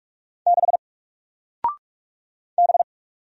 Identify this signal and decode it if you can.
Morse code, heard as a keyed tone, about 44 wpm, 700 Hz. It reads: 6 B
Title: Single Letter-Number 45wpm